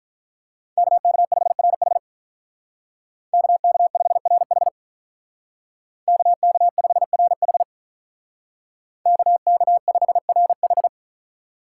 Morse code, heard as a keyed tone, about 46 wpm, 700 Hz. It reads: KK5RH KK5RH KK5RH KK5RH